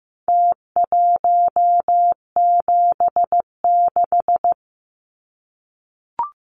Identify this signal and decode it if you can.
Morse code, heard as a keyed tone, about 15 wpm, 700 Hz. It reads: T176
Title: Numbers - Contest Style - 4-Digit 15wpm